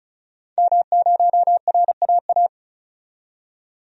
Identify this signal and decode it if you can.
Morse code, heard as a keyed tone, about 36 wpm, 700 Hz. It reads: M0RAA
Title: DXCC-100 Call Signs - 5 Characters - 35wpm